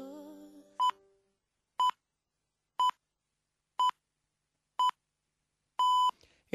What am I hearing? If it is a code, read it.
Radio time signal (time pips). It is heard from 0.8 to 6.1 s.